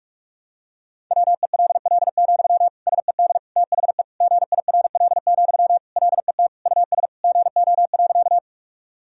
Morse code, heard as a keyed tone, about 41 wpm, 700 Hz. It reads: WELL, SED THE GIRL, LET US GO.
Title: Sentences from Top 400 Words and Abbreviations 45wpm